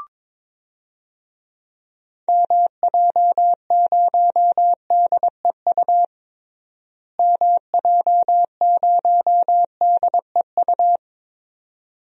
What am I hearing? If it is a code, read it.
Morse code, heard as a keyed tone, about 22 wpm, 700 Hz. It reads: MJ0DEU MJ0DEU
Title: DXCC-100 Call Signs - 6 Characters - Repeated 2x - 22wpm